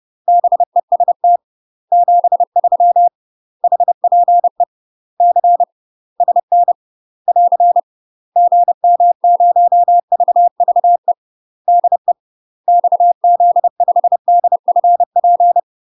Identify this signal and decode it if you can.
Morse code, heard as a keyed tone, about 30 wpm, 700 Hz. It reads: BEST 73 HPE C SN <AR> GM0VVE DE XZ5DFP
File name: UK MASTER QSOs TRAINING / 30WPM / UK MASTER QSOs 0017